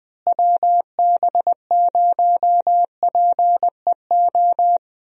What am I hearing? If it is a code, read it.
Morse code, heard as a keyed tone, about 21 wpm, 700 Hz. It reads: WB0PEO
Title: USA Call Signs - 6 Characters - Repeated 2x - 20wpm